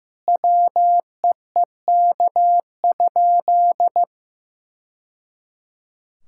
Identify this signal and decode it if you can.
Morse code, heard as a keyed tone, about 15 wpm, 700 Hz. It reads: WEEK?
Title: Sentences from Top 1000 Words - Review Part 5 - 15wpm